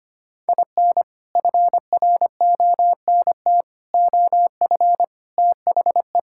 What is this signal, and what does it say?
Morse code, heard as a keyed tone, about 25 wpm, 700 Hz. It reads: IN FRONT OF THE
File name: practice / 25wpm / Sentences with 4-Word Phrases - 25wpm